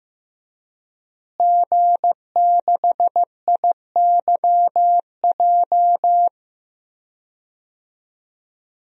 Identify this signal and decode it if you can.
Morse code, heard as a keyed tone, about 15 wpm, 700 Hz. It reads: G6IYJ
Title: DXCC-100 Call Signs - 5 Characters - Repeated 3x - 15wpm